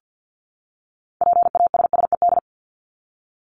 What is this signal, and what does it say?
Morse code, heard as a keyed tone, about 50 wpm, 700 Hz. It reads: PASSED
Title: ED Suffix - Top 100 Words with Base Words 50wpm